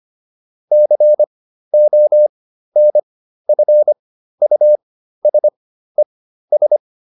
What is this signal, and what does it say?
Morse code, heard as a keyed tone, about 25 wpm, spaced as for 13 wpm, 600 Hz. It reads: CONFUSES